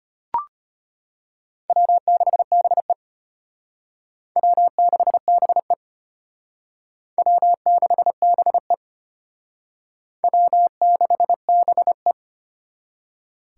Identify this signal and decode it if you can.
Morse code, heard as a keyed tone, about 32 wpm, 700 Hz. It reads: W6BE W6BE W6BE W6BE